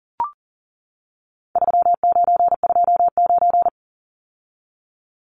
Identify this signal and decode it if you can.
Morse code, heard as a keyed tone, about 40 wpm, 700 Hz. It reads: 3929
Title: Numbers - 4-Digit 40wpm